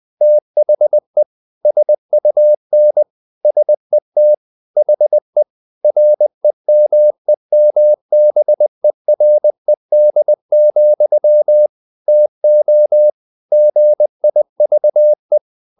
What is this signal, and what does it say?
Morse code, heard as a keyed tone, about 20 wpm, 600 Hz. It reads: THE SUN SET HE REMEMBERED, TO GIVE
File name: the old man and the sea / the old man and the sea part 1 20wpm